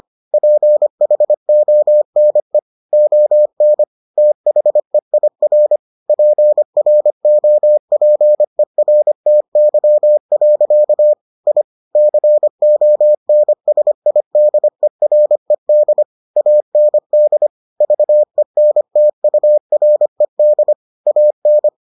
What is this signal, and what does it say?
Morse code, heard as a keyed tone, about 25 wpm, 600 Hz. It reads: PHONE ON THEIR PROPERTY. I CONSIDERED AND VENTURED AN